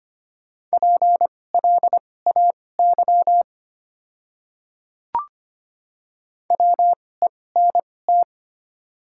Morse code, heard as a keyed tone, about 25 wpm, spaced as for 18 wpm, 700 Hz. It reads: PLAY WENT
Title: 2x Character Spacing - Top 200 Words - 25wpm